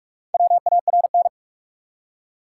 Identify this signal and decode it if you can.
Morse code, heard as a keyed tone, about 42 wpm, 700 Hz. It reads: WARN
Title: Four Letter Words - 45wpm